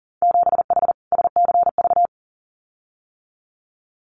Morse code, heard as a keyed tone, about 38 wpm, 700 Hz. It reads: 7H SCV
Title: Winter Field Day Exchanges Encore 40wpm